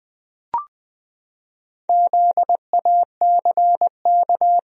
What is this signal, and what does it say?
Morse code, heard as a keyed tone, about 20 wpm, 700 Hz. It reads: ZACK
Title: USA Common Names - Max 4 Letters - 20wpm